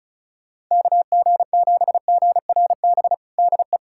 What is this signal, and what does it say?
Morse code, heard as a keyed tone, about 36 wpm, 700 Hz. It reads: KG7GRB DE
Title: USA Callsign DE Callsign - 6 Characters - 35wpm